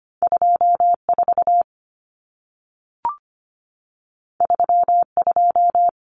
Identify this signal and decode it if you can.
Morse code, heard as a keyed tone, about 25 wpm, 700 Hz. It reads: 24 32
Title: Numbers - 2-Digit 25wpm